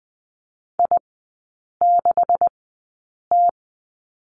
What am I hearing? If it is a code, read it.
Morse code, heard as a keyed tone, about 20 wpm, spaced as for 8 wpm, 700 Hz. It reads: I6T
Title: 127 Lesson 21e - Number 6 - Three Characters 20wpm